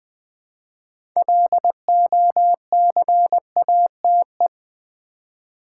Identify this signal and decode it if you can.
Morse code, heard as a keyed tone, about 20 wpm, 700 Hz. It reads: LOCATE
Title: Build A Word - 2 Letters At A Time - 6-Letter Words - 20wpm